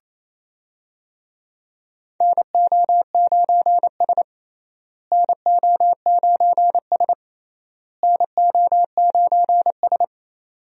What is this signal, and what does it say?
Morse code, heard as a keyed tone, about 28 wpm, 700 Hz. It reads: NO9S NO9S NO9S
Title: USA Call Signs - 4 Characters - Repeated 3x - 28wpm